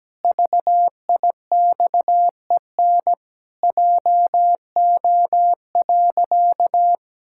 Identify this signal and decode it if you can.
Morse code, heard as a keyed tone, about 17 wpm, 700 Hz. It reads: VIXEN JO.
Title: Pangram Challenge v2 - 17wpm